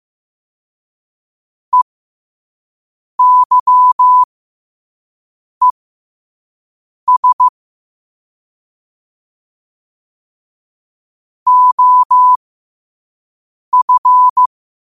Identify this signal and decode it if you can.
Morse code, heard as a keyed tone, about 15 wpm, spaced as for 5 wpm, 1000 Hz. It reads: EYES OF
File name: qotdfiles / 5wpm